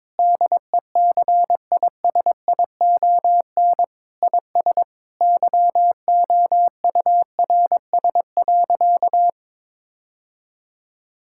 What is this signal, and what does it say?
Morse code, heard as a keyed tone, about 22 wpm, 700 Hz. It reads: DECISION IS YOURS.
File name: practice / 22wpm / Sentences with 8-Word Phrases - 22wpm